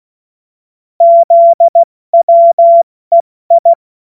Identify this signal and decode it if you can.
Morse code, heard as a keyed tone, about 16 wpm, spaced as for 14 wpm, 680 Hz. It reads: ZWEI
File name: Morsekurs / Witze / Witz 15. Februar 2022 70 F80